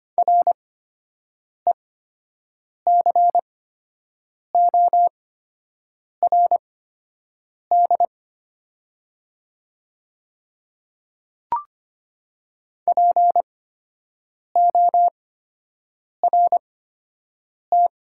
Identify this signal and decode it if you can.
Morse code, heard as a keyed tone, about 25 wpm, spaced as for 7 wpm, 700 Hz. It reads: RECORD PORT
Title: 8x Character Spacing - Top 500 Words - 25wpm